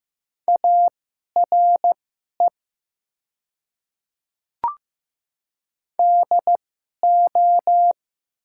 Morse code, heard as a keyed tone, about 15 wpm, spaced as for 11 wpm, 700 Hz. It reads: ARE DO
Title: 2x Character Spacing - Top 100 Words - 15wpm